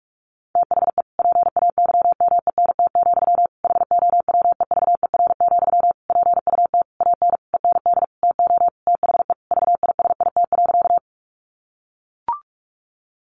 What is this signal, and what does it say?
Morse code, heard as a keyed tone, about 44 wpm, 700 Hz. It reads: THE PAYMENT, HOWEVER, PUT AN END TO THE VISIT.
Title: Sentences from Top 1200 Words 45wpm